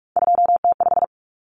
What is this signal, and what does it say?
Morse code, heard as a keyed tone, about 44 wpm, 700 Hz. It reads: 2T5
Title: Common Cut Numbers - 3-Digit 45wpm